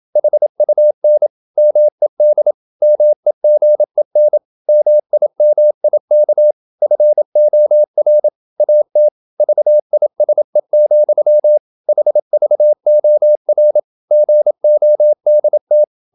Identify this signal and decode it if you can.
Morse code, heard as a keyed tone, about 27 wpm, 600 Hz. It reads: HUN MED MEGEN MIMIK FOR AT VISE, HVOR GODT
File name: Ferie QRQ 27wpm